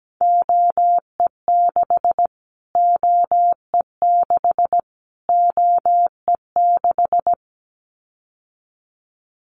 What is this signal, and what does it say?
Morse code, heard as a keyed tone, about 17 wpm, 700 Hz. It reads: OE6 OE6 OE6
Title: Callsign Prefixes for Top 25 DXCC - Repeated 3x - 17wpm